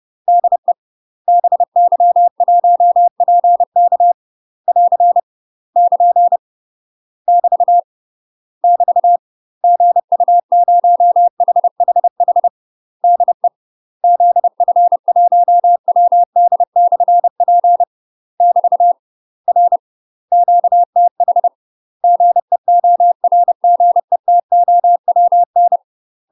Morse code, heard as a keyed tone, about 30 wpm, 700 Hz. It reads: DE BY1PK <AR> <KN> <BT> <BT> GU0HHH DE ZF1WD/P <BT> R QTH GEORGETOWN
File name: UK MASTER QSOs TRAINING / 30WPM / UK MASTER QSOs 0023